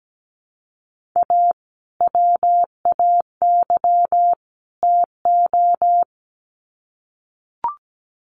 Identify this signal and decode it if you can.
Morse code, heard as a keyed tone, about 17 wpm, 700 Hz. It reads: A WAY TO
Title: Sets of 3 Words 17wpm